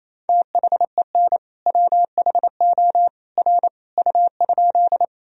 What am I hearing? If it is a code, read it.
Morse code, heard as a keyed tone, about 28 wpm, 700 Hz. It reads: THEN WHO R U?
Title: Sentences from Top 100 Words and Abbreviations 28wpm